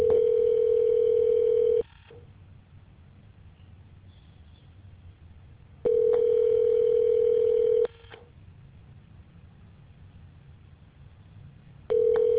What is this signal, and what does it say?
Ringback tone (North America), 440 + 480 Hz.